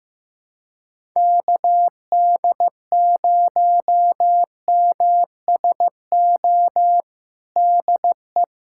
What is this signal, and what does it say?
Morse code, heard as a keyed tone, about 15 wpm, 700 Hz. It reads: KD0MSO DE
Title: USA Callsign DE Callsign - 6 Characters - 15wpm